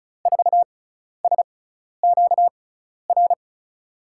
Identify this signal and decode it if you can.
Morse code, heard as a keyed tone, about 36 wpm, spaced as for 12 wpm, 700 Hz. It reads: 4SQR
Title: USA Call Signs - 6 Characters - 6x Character Spacing - 35wpm